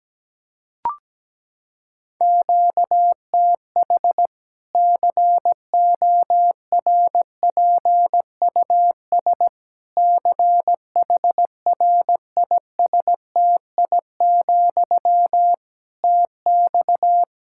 Morse code, heard as a keyed tone, about 17 wpm, 700 Hz. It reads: QTH CORPUS CHRISTI, TX
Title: Practice QSO - Exchange 1 - Beginning - 17wpm